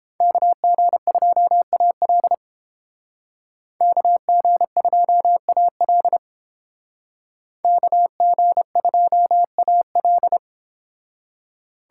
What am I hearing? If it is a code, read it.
Morse code, heard as a keyed tone, about 29 wpm, 700 Hz. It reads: KG2AL KG2AL KG2AL